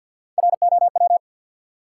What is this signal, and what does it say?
Morse code, heard as a keyed tone, about 52 wpm, 700 Hz. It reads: ROW
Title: Three Letter Words - 50wpm